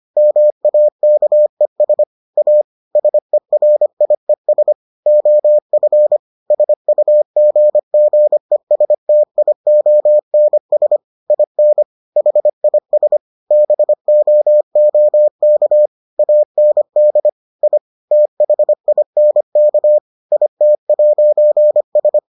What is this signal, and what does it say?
Morse code, heard as a keyed tone, about 25 wpm, 600 Hz. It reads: MAKES A SERIES OF SUGGESTIONS IN HIS BOOK AND I THINK IT'S